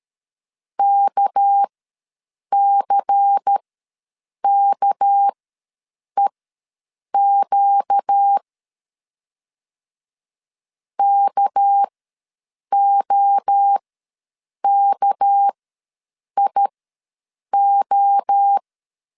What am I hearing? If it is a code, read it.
Morse code, heard as a keyed tone, about 13 wpm, spaced as for 7 wpm, 800 Hz. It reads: KCKEQ KOKIO
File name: LEZIONI / 6 PKX0 / 6 PK ws 12wpm 3116